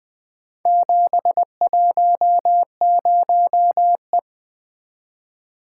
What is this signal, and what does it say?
Morse code, heard as a keyed tone, about 20 wpm, 700 Hz. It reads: 710E
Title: Four Letter-Number Combinations - 20wpm